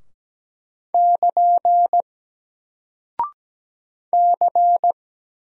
Morse code, heard as a keyed tone, about 17 wpm, 700 Hz. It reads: <KN> C
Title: Single Character (Letter:Number:Punctuation:Prosign) Alt 17wpm